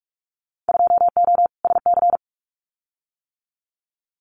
Morse code, heard as a keyed tone, about 46 wpm, 700 Hz. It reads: 2O SC
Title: Winter Field Day Exchanges Encore 45wpm